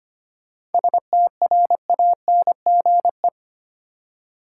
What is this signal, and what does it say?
Morse code, heard as a keyed tone, about 25 wpm, 700 Hz. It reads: STRANGE
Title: Seven Letter Words - 25wpm